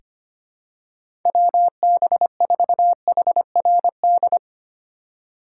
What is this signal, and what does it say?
Morse code, heard as a keyed tone, about 25 wpm, 700 Hz. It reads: WB4HRD
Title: USA Call Signs - 6 Characters - 4x Character Spacing - 25wpm